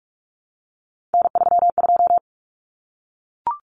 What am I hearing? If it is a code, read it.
Morse code, heard as a keyed tone, about 44 wpm, 700 Hz. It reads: N32
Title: Common Cut Numbers - 3-Digit 45wpm